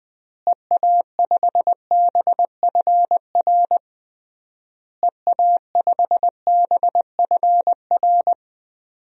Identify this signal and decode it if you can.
Morse code, heard as a keyed tone, about 20 wpm, 700 Hz. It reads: EA5BFR EA5BFR
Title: DXCC-100 Call Signs - 6 Characters - Repeated 3x - 20wpm